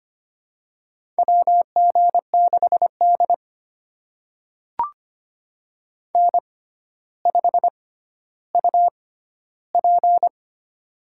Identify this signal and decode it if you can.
Morse code, heard as a keyed tone, about 25 wpm, spaced as for 13 wpm, 700 Hz. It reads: WG6D N5UP
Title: USA Call Signs - 4 Characters - 6x Character Spacing - 25wpm